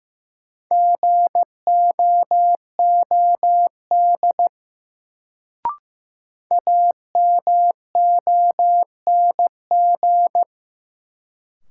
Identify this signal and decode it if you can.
Morse code, heard as a keyed tone, about 15 wpm, 700 Hz. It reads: GOOD AMONG
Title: Sets of 2 Words - New Word Drill - 15wpm